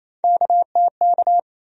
Morse code, heard as a keyed tone, about 28 wpm, 700 Hz. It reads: KTK
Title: USA Call Sign Postfixes - 3 Letters 28wpm